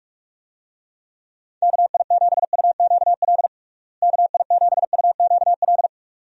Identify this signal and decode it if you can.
Morse code, heard as a keyed tone, about 48 wpm, 700 Hz. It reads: KI7UQL KI7UQL
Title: USA Call Signs - 6 Characters - Repeated 2x - 45wpm